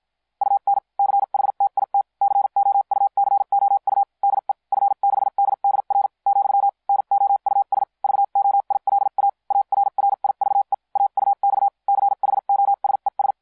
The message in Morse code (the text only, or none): UN CHTIT COUCOU DE F6DDR , NOUS VOILA ARRIVE AUX CHOSES